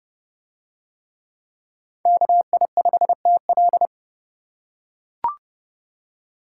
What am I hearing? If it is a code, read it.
Morse code, heard as a keyed tone, about 30 wpm, 700 Hz. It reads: KI5TL